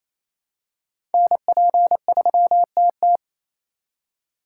Morse code, heard as a keyed tone, about 27 wpm, 700 Hz. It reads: NP3TT
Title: USA Call Signs - 5 Characters - 8x Character Spacing - 28wpm